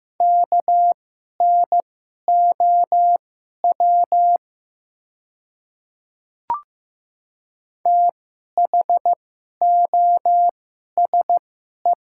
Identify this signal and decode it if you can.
Morse code, heard as a keyed tone, about 15 wpm, spaced as for 11 wpm, 700 Hz. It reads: KNOW THOSE
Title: 2x Character Spacing - Top 500 Words - 15wpm